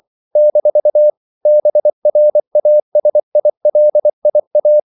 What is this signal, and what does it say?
Morse code, heard as a keyed tone, about 24 wpm, 600 Hz. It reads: - BRASILIA
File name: CW / Practice / NationalCapitals24